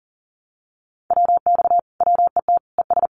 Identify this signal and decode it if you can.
Morse code, heard as a keyed tone, about 40 wpm, 700 Hz. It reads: WX WET ES
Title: Common QSO Phrases Encore 40wpm